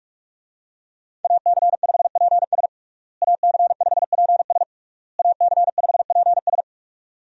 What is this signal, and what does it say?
Morse code, heard as a keyed tone, about 41 wpm, 700 Hz. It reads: AC5PS AC5PS AC5PS
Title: USA Call Signs - 5 Characters - Repeated 3x - 45wpm